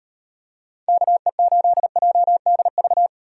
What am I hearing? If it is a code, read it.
Morse code, heard as a keyed tone, about 38 wpm, 700 Hz. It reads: KE8JDV